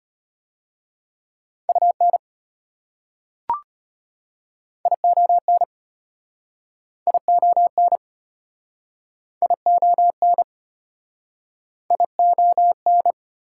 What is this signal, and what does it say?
Morse code, heard as a keyed tone, about 33 wpm, 700 Hz. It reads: UN ION ION ION ION